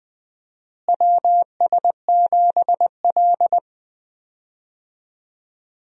Morse code, heard as a keyed tone, about 20 wpm, 700 Hz. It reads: WS7L
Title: USA Call Signs - 4 Characters - Repeated 2x - 20wpm